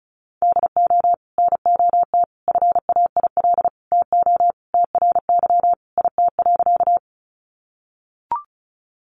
Morse code, heard as a keyed tone, about 34 wpm, 700 Hz. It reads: DO NOT FAIL TO TRY IT.
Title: Sentences from Top 1100 Words 35wpm